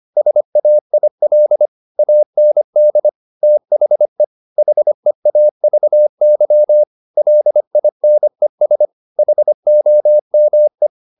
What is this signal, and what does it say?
Morse code, heard as a keyed tone, about 25 wpm, 600 Hz. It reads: SAIL AND THE HEAVY LINES HOME
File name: the old man and the sea part 1 25wpm